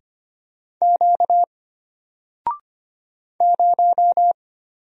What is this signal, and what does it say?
Morse code, heard as a keyed tone, about 25 wpm, 700 Hz. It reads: Q 0
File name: Single-Letter-Number-25wpm